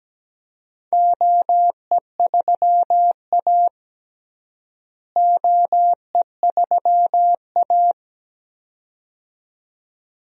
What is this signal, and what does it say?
Morse code, heard as a keyed tone, about 17 wpm, 700 Hz. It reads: OE3A OE3A
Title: DXCC-100 Call Signs - 4 Characters - Repeated 2x - 17wpm